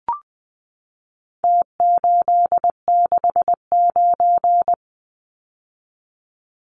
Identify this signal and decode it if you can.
Morse code, heard as a keyed tone, about 20 wpm, 700 Hz. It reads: T869
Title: Numbers - Contest Style - 4-Digit 20wpm